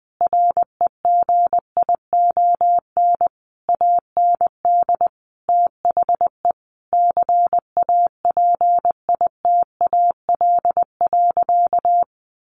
Morse code, heard as a keyed tone, about 20 wpm, 700 Hz. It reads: REGION AND THE CAPITAL.
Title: Sentences from Top 1500 Words 20wpm